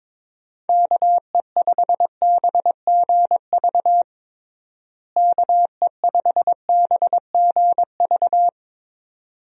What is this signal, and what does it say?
Morse code, heard as a keyed tone, about 22 wpm, 700 Hz. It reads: KE5BGV KE5BGV
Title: USA Call Signs - 6 Characters - Repeated 2x - 22wpm